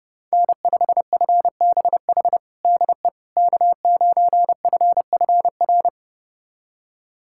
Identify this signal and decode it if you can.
Morse code, heard as a keyed tone, about 30 wpm, 700 Hz. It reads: N5FBH DE K9FFR
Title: USA Callsign DE Callsign - 5 Characters - 30wpm